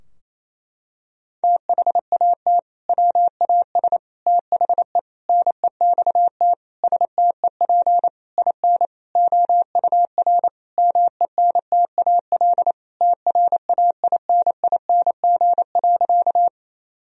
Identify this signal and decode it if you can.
Morse code, heard as a keyed tone, about 28 wpm, 700 Hz. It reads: THAT WAS THE NEXT STEP IN OUR MENTAL TRAINING.
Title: Sentences from Top 1000 Words - Review Part 4 - 28wpm